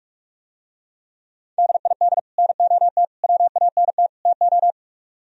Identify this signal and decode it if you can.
Morse code, heard as a keyed tone, about 49 wpm, 700 Hz. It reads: DID NOT WANT TO